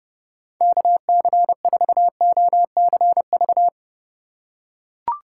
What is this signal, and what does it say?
Morse code, heard as a keyed tone, about 30 wpm, 700 Hz. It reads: KC4OCV
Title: USA Call Signs - 6 Characters - 8x Character Spacing - 30wpm